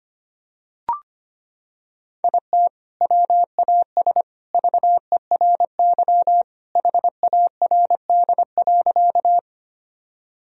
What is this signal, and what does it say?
Morse code, heard as a keyed tone, about 25 wpm, 700 Hz. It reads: IT WAS VERY HARD.
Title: Sentences from Top 1000 Words - Review Part 1 - 25wpm